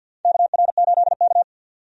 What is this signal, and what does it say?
Morse code, heard as a keyed tone, about 46 wpm, 700 Hz. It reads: KR8K
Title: USA Call Signs - 4 Characters - 50wpm